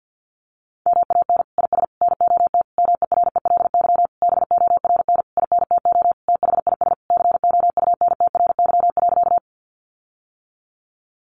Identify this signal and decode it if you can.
Morse code, heard as a keyed tone, about 48 wpm, 700 Hz. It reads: MAN IS NOT MERELY BORN INTO THIS COUNTRY.
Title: Sentences from Top 1400 Words 50wpm